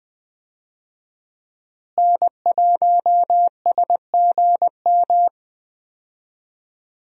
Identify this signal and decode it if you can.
Morse code, heard as a keyed tone, about 20 wpm, 700 Hz. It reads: N1SGM
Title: USA Call Signs - 5 Characters - Repeated 2x - 20wpm